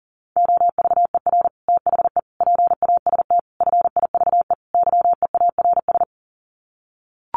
Morse code, heard as a keyed tone, about 40 wpm, 700 Hz. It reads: OVER THE PAST FIVE YEARS
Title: Sets of 5 Words Encore 40wpm